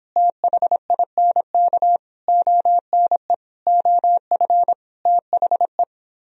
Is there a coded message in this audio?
Yes: Morse code from 0.2 to 5.8 s, heard as a keyed tone, about 29 wpm, 700 Hz. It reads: THINK ONE OF THE